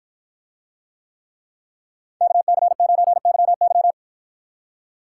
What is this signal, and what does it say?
Morse code, heard as a keyed tone, about 53 wpm, 700 Hz. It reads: KC9YY